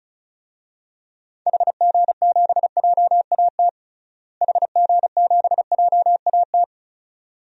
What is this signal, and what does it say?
Morse code, heard as a keyed tone, about 36 wpm, 700 Hz. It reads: HG7JAT HG7JAT
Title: DXCC-100 Call Signs - 6 Characters - Repeated 2x - 35wpm